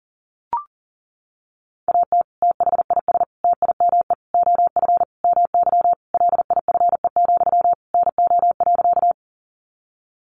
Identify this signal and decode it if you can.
Morse code, heard as a keyed tone, about 40 wpm, 700 Hz. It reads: AT THIS TIME OF MY LIFE, NO.
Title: Sentences from Top 300 Words 40wpm